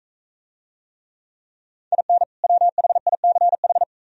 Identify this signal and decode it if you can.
Morse code, heard as a keyed tone, about 40 wpm, 700 Hz. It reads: IN WHICH